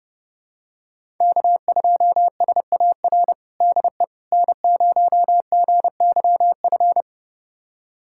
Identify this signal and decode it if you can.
Morse code, heard as a keyed tone, about 31 wpm, 700 Hz. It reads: K2SAR DE N0GYF
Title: USA Callsign DE Callsign - 5 Characters - 30wpm